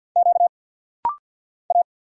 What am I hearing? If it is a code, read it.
Morse code, heard as a keyed tone, about 50 wpm, 700 Hz. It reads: Q A
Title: Single Letter-Number Alt - Rapid-Fire 50wpm